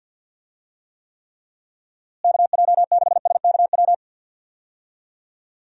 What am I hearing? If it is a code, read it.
Morse code, heard as a keyed tone, about 53 wpm, 700 Hz. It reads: KJ6SKW
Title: USA Call Signs - 6 Characters - Repeated 3x - 50wpm